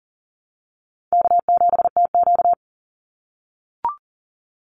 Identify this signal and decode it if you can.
Morse code, heard as a keyed tone, about 40 wpm, 700 Hz. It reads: K7TQ
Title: NA SOTA Chaser Callsigns 40wpm